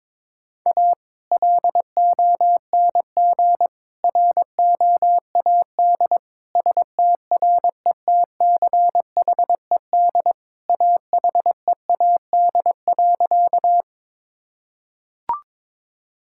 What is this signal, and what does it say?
Morse code, heard as a keyed tone, about 22 wpm, 700 Hz. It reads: A LONG ROAD STRETCHED AHEAD.
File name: practice / 22wpm / Sentences with 2-Word Phrases - 22wpm